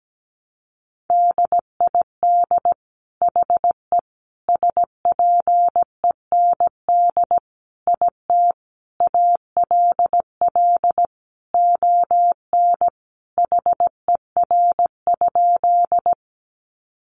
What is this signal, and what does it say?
Morse code, heard as a keyed tone, about 17 wpm, 700 Hz. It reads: DID HE SPEND IT ALL ON HER?
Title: Sentences from Top 800 Words 17wpm